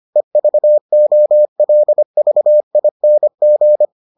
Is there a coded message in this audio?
Yes: Morse code from 0.2 to 3.9 s, heard as a keyed tone, about 25 wpm, 600 Hz. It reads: EVOLVING